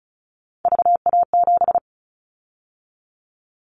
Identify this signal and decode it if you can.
Morse code, heard as a keyed tone, about 36 wpm, 700 Hz. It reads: VA7
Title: Callsign Prefixes for Top 50 DXCC - 35wpm